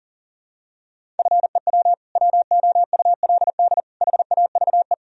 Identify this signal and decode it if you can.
Morse code, heard as a keyed tone, about 37 wpm, 700 Hz. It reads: FEW WOULD HAVE
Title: Sentences from Top 1000 Words - Review Part 1 - 40wpm